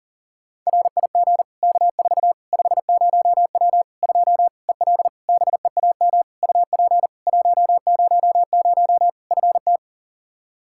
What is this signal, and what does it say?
Morse code, heard as a keyed tone, about 40 wpm, 700 Hz. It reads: RIG K4 50W 2 EL BEAM UP 100 FT
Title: Practice QSO - Exchange 2 - Info - 40wpm